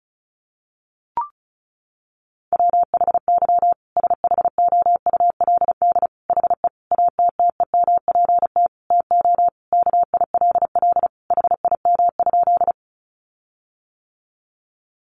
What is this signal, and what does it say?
Morse code, heard as a keyed tone, about 34 wpm, 700 Hz. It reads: WHY SHOULD HE ATTEMPT TO KILL HIM?
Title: Sentences from Top 1400 Words 35wpm